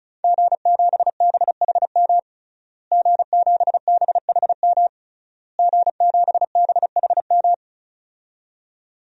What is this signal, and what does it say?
Morse code, heard as a keyed tone, about 34 wpm, 700 Hz. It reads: G7BHM G7BHM G7BHM
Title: DXCC-100 Call Signs - 5 Characters - Repeated 3x - 35wpm